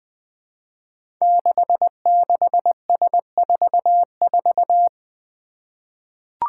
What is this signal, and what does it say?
Morse code, heard as a keyed tone, about 20 wpm, 700 Hz. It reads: 66S44
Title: Five Challenging Letter-Number Combinations 20wpm